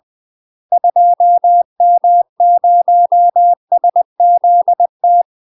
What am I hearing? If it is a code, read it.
Morse code, heard as a keyed tone, about 20 wpm, 700 Hz. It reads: 2M0SZT
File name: UK MASTER QSOs TRAINING / 20WPM / UK MASTER QSOs 0065